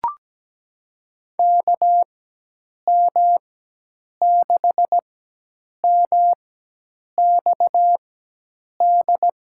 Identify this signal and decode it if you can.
Morse code, heard as a keyed tone, about 17 wpm, spaced as for 8 wpm, 700 Hz. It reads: KM6MXD
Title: USA Call Signs - 4x Character Spacing - 17wpm